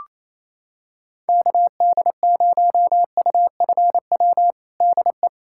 Morse code, heard as a keyed tone, about 28 wpm, 700 Hz. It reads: KD0UFW DE
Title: USA Callsign DE Callsign - 6 Characters - 28wpm